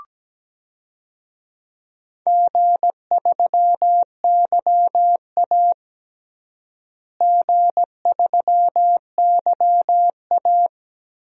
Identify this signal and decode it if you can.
Morse code, heard as a keyed tone, about 17 wpm, 700 Hz. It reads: G3YA G3YA
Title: DXCC-100 Call Signs - 4 Characters - Repeated 3x - 17wpm